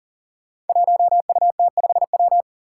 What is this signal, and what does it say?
Morse code, heard as a keyed tone, about 38 wpm, 700 Hz. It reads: 1UT5W
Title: Five Challenging Letter-Number Combinations 40wpm